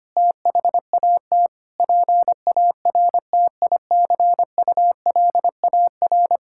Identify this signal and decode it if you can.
Morse code, heard as a keyed tone, about 25 wpm, 700 Hz. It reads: THAT PARTICULAR
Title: Sentences from Top 1000 Words - Review Part 5 - 25wpm